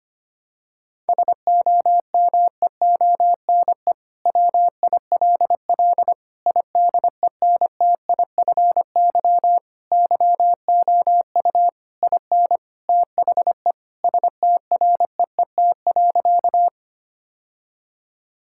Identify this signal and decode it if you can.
Morse code, heard as a keyed tone, about 25 wpm, 700 Hz. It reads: SOMEONE WILL IDENTIFY YOU IN THE STREET.
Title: Sentences from Top 1000 Words - Review Part 6 - 25wpm